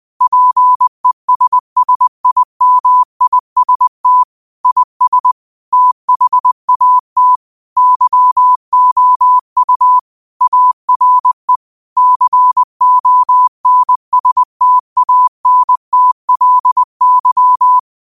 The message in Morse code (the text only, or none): PESSIMIST IS THAT YOU ARE CONSTANTLY